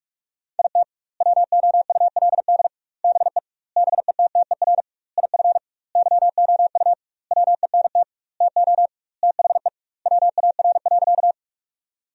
Morse code, heard as a keyed tone, about 46 wpm, 700 Hz. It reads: IT WOULD BE BETTER IF YOU WENT TO THE WAR.
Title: 2x Word-Spacing - Sentences from Top 400 Words - 45wpm